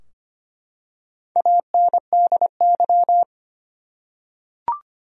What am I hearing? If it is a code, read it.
Morse code, heard as a keyed tone, about 25 wpm, 700 Hz. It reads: ANDY
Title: USA Common Names - 25wpm